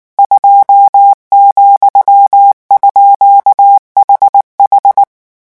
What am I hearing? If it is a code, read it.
Morse code, heard as a keyed tone, about 19 wpm, 800 Hz. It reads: 2,_HH
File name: morse mp3 / revision14-vitesse 18 mots